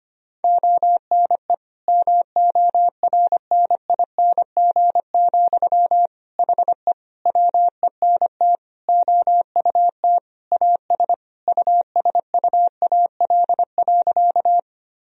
Morse code, heard as a keyed tone, about 25 wpm, 700 Hz. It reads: ONE MORNING, HE WENT OUT AS USUAL.
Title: Sentences from Top 1000 Words - Review Part 2 - 25wpm